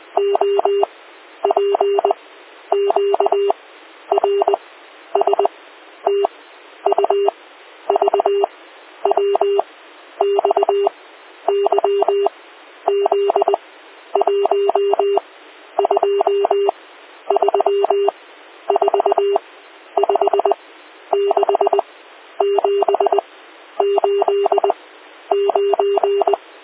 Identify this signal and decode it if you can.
Morse code, heard as a keyed tone, about 20 wpm, 400 Hz. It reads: O P Q R S T U V W X Y Z 1 2 3 4 5 6 7 8 9